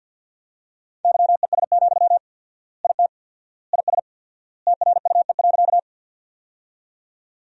Morse code, heard as a keyed tone, about 48 wpm, 700 Hz. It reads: YES, IT IS TRUE.
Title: 4x Word-Spacing - Sentences from Top 500 Words - 50wpm